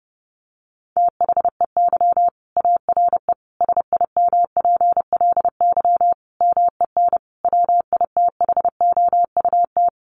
Morse code, heard as a keyed tone, about 30 wpm, 700 Hz. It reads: THEY ARE SIMPLY MEN WITHOUT
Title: Sentences from Top 1500 Words 30wpm